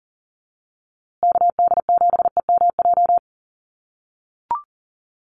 Morse code, heard as a keyed tone, about 40 wpm, 700 Hz. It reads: KD7EMJ
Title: USA Call Signs Repeated 3x 40wpm